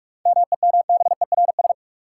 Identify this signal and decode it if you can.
Morse code, heard as a keyed tone, about 46 wpm, 700 Hz. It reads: MEMBERS
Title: Sets of 3 Words - New Word Drill - 45wpm